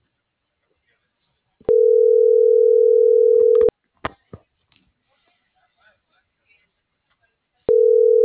Ringback tone (North America), 440 + 480 Hz.